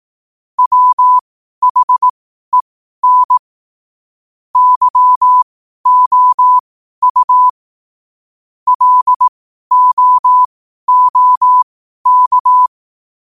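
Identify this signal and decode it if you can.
Morse code, heard as a keyed tone, about 18 wpm, spaced as for 12 wpm, 1000 Hz. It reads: WHEN YOU LOOK
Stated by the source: Morse code sent at 12 words per minute.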